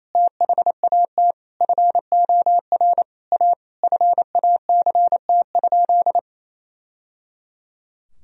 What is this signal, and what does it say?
Morse code, heard as a keyed tone, about 28 wpm, 700 Hz. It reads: THAT FOR A FACT?
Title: Sentences from Top 1000 Words - Review Part 3 - 28wpm